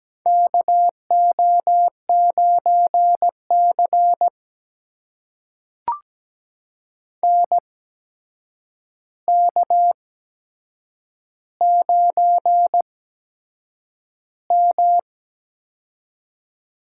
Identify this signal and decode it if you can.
Morse code, heard as a keyed tone, about 17 wpm, spaced as for 7 wpm, 700 Hz. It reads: KO9C NK9M
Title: USA Call Signs - 8x Character Spacing - 17wpm